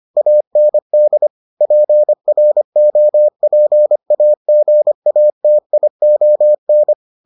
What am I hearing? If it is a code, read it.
Morse code, heard as a keyed tone, about 25 wpm, 600 Hz. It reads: AND PROPAGATION